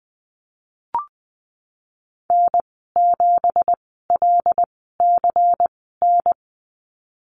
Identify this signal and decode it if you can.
Morse code, heard as a keyed tone, about 20 wpm, spaced as for 15 wpm, 700 Hz. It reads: N7LCN
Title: 179 Lesson 29g - Slash - Call Signs 20wpm